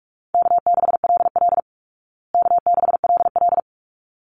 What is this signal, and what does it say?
Morse code, heard as a keyed tone, about 44 wpm, 700 Hz. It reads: K6LL K6LL
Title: Call Signs Repeated 3x - 45wpm